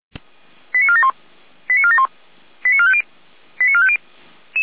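Five-tone selective call (ZVEI), heard as quick runs of five tones, tones of about 70 ms. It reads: 89471 89471 89450 89450
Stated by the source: ZVEI five-tone selective call received by SDR, feuerwehr-orlamuende.org fire brigade audio archive.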